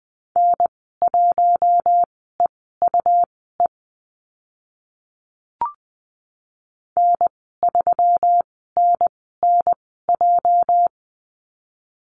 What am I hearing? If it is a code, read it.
Morse code, heard as a keyed tone, about 20 wpm, spaced as for 14 wpm, 700 Hz. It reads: N1EUE N3NNJ
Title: 238 Lesson 38g - Letter q - Call Signs 20wpm